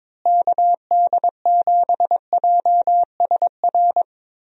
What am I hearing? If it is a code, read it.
Morse code, heard as a keyed tone, about 22 wpm, 700 Hz. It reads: KD7JSR
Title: USA Call Signs - 6 Characters - Repeated 3x - 22wpm